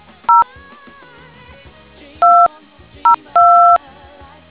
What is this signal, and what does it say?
DTMF, heard as telephone touch tones, tones of about 160 ms. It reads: 0 202